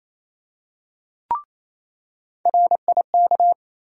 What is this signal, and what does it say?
Morse code, heard as a keyed tone, about 28 wpm, 700 Hz. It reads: RIK
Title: USA Call Sign Postfixes - 3 Letters 28wpm